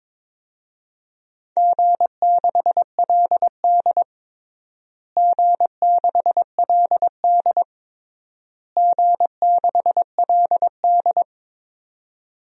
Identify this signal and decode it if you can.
Morse code, heard as a keyed tone, about 22 wpm, 700 Hz. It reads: G6LD G6LD G6LD
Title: DXCC-100 Call Signs - 4 Characters - Repeated 3x - 22wpm